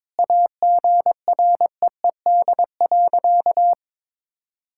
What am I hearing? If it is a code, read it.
Morse code, heard as a keyed tone, about 22 wpm, 700 Hz. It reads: AGREED.
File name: practice / 22wpm / Sentences with 3-Word Phrases - 22wpm